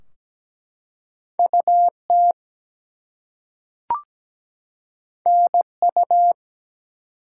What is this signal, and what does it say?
Morse code, heard as a keyed tone, about 17 wpm, 700 Hz. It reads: UT NU
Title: US States and Canadian Provinces 17wpm